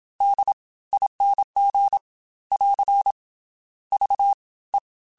Morse code, heard as a keyed tone, about 27 wpm, spaced as for 19 wpm, 800 Hz. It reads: DING <AR> VE